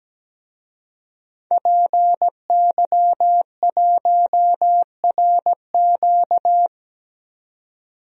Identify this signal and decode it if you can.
Morse code, heard as a keyed tone, about 17 wpm, 700 Hz. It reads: PY1RQ
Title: DXCC-100 Call Signs - 5 Characters - 17wpm